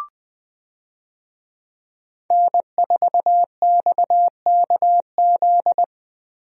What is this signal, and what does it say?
Morse code, heard as a keyed tone, about 20 wpm, 700 Hz. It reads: N4XKZ
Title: USA Call Signs - 5 Characters - Repeated 2x - 20wpm